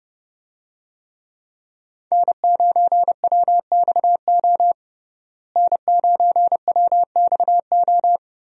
Morse code, heard as a keyed tone, about 30 wpm, 700 Hz. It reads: N9WXO N9WXO
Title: USA Call Signs - 5 Characters - Repeated 3x - 30wpm